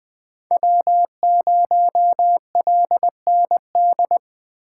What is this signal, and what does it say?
Morse code, heard as a keyed tone, about 21 wpm, 700 Hz. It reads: W0LND